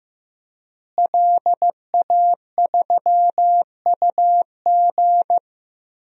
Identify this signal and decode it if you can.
Morse code, heard as a keyed tone, about 15 wpm, 700 Hz. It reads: LA3UG
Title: DXCC-100 Call Signs - 5 Characters - Repeated 2x - 15wpm